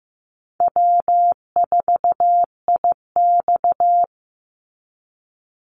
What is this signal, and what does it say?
Morse code, heard as a keyed tone, about 15 wpm, 700 Hz. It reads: W4IX
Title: Easy Call Signs - 15wpm